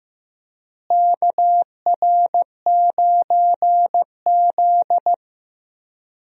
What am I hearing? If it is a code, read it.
Morse code, heard as a keyed tone, about 15 wpm, 700 Hz. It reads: KR9Z
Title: USA Call Signs - 4 Characters - Repeated 3x - 15wpm